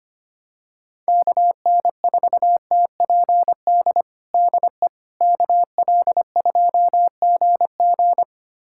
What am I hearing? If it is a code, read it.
Morse code, heard as a keyed tone, about 25 wpm, 700 Hz. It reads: KN4TPD DE KL2GG
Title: USA Callsign DE Callsign 25wpm